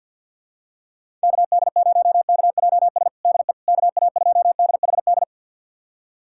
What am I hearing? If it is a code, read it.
Morse code, heard as a keyed tone, about 50 wpm, 700 Hz. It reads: KD0KJS DE KA2DHD
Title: USA Callsign DE Callsign - 6 Characters - 50wpm